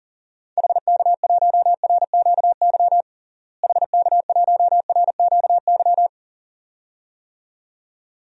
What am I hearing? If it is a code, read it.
Morse code, heard as a keyed tone, about 40 wpm, 700 Hz. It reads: HK1RQY HK1RQY
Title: DXCC-100 Call Signs - 6 Characters - Repeated 2x - 40wpm